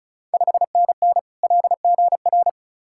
Morse code, heard as a keyed tone, about 36 wpm, 700 Hz. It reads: 5NN LGR
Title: Ontario QSO Party Exchange 35wpm